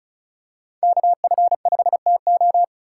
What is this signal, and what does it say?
Morse code, heard as a keyed tone, about 36 wpm, 700 Hz. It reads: KF5TO
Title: USA Call Signs - 5 Characters - 4x Character Spacing - 35wpm